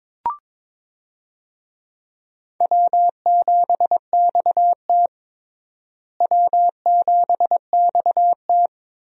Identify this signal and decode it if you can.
Morse code, heard as a keyed tone, about 22 wpm, 700 Hz. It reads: W7XT W7XT
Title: USA Call Signs - 4 Characters - Repeated 2x - 22wpm